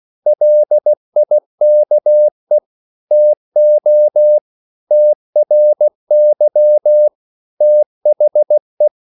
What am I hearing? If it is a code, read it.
Morse code, heard as a keyed tone, about 16 wpm, 600 Hz. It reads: LIKE TO TRY THE